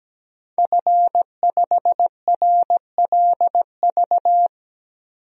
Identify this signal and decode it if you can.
Morse code, heard as a keyed tone, about 17 wpm, 700 Hz. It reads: F5RLV
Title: DXCC-100 Call Signs - 5 Characters - 17wpm